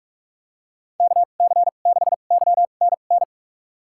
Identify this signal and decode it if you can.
Morse code, heard as a keyed tone, about 46 wpm, spaced as for 32 wpm, 700 Hz. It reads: KC6YNN
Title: USA Call Signs - 6 Characters - 2x Character Spacing - 45wpm